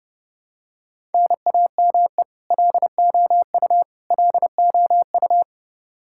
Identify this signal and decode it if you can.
Morse code, heard as a keyed tone, about 30 wpm, 700 Hz. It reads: NAME LOU LOU
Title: Practice QSO - Exchange 1 - Beginning - 30wpm